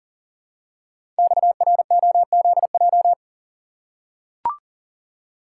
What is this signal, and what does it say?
Morse code, heard as a keyed tone, about 38 wpm, 700 Hz. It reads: XROZJ
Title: Code Groups - 5 Letters - 40wpm